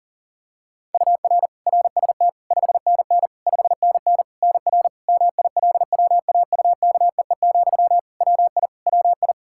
Morse code, heard as a keyed tone, about 40 wpm, 700 Hz. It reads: UR RST 5NN 5NN NR MILWAUKEE, WI WI
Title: Practice QSO - Exchange 1 - Beginning - 40wpm